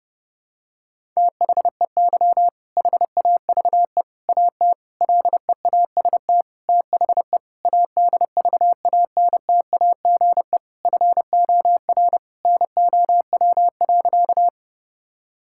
Morse code, heard as a keyed tone, about 30 wpm, 700 Hz. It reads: THEY HAVE AT LEAST THE ADVANTAGE FOR NOW.
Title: Sentences from Top 1000 Words - Review Part 6 - 30wpm